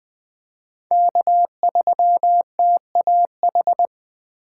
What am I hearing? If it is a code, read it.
Morse code, heard as a keyed tone, about 20 wpm, 700 Hz. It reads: K3TAH
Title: USA Call Signs - 5 Characters - 6x Character Spacing - 20wpm